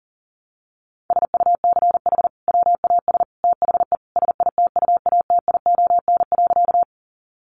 Morse code, heard as a keyed tone, about 40 wpm, 700 Hz. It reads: SUCH WAS THE SITUATION.
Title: Sentences from Top 1200 Words 40wpm